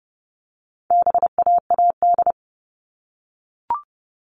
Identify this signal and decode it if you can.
Morse code, heard as a keyed tone, about 30 wpm, 700 Hz. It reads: BAAD
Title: All Cut Numbers - 4-Digit 30wpm